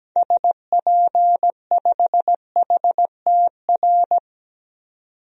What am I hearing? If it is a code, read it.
Morse code, heard as a keyed tone, about 17 wpm, 700 Hz. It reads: SP5HTR
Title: DXCC-100 Call Signs - 6 Characters - 17wpm